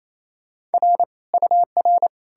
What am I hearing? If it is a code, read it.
Morse code, heard as a keyed tone, about 37 wpm, 700 Hz. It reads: R UR